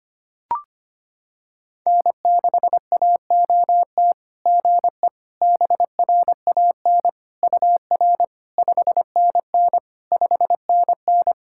Morse code, heard as a keyed tone, about 25 wpm, 690 Hz. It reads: N6AOT GE BRAN UR 5NN 5NN
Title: SOTA QSO Practice 25wpm